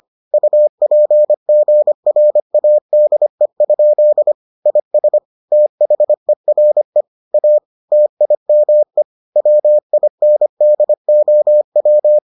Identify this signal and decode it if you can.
Morse code, heard as a keyed tone, about 25 wpm, 600 Hz. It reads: UPGRADE? IS THERE A TIME WINDOW